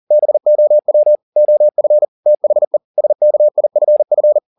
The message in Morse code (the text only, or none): BOW OF THE SKIFF